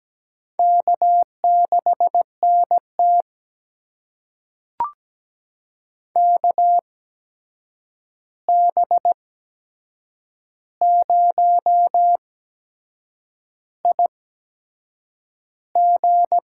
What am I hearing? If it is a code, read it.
Morse code, heard as a keyed tone, about 17 wpm, spaced as for 7 wpm, 700 Hz. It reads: K6NT KB0IG
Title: USA Call Signs - 8x Character Spacing - 17wpm